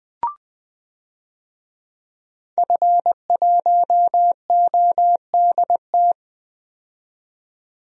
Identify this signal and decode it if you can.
Morse code, heard as a keyed tone, about 20 wpm, 700 Hz. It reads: F1ODT
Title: DXCC-100 Call Signs - 5 Characters - 20wpm